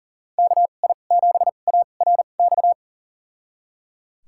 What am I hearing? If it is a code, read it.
Morse code, heard as a keyed tone, about 37 wpm, spaced as for 28 wpm, 700 Hz. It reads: KI7ARX
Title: USA Call Signs - 6 Characters - 2x Character Spacing - 40wpm